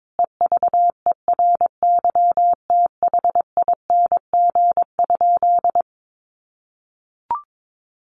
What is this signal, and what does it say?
Morse code, heard as a keyed tone, about 22 wpm, 700 Hz. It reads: EVERYTHING?
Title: Sentences from Top 700 Words 22wpm